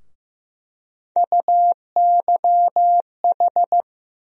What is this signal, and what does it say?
Morse code, heard as a keyed tone, about 15 wpm, 700 Hz. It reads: UYH
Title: USA Call Sign Postfixes - 3 Letters 15wpm